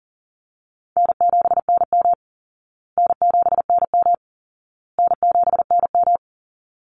Morse code, heard as a keyed tone, about 40 wpm, 700 Hz. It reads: N7NM N7NM N7NM
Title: Call Signs Repeated 2x - 40wpm